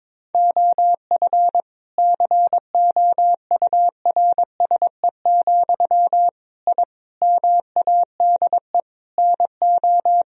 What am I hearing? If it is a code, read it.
Morse code, heard as a keyed tone, about 22 wpm, 700 Hz. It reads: OF COURSE, I MADE NO
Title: Sentences from Top 1000 Words - Review Part 1 - 22wpm